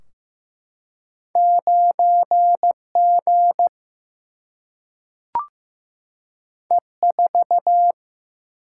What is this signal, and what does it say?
Morse code, heard as a keyed tone, about 15 wpm, 700 Hz. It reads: 9 G E 4
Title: Single-Letter Single-Number 15wpm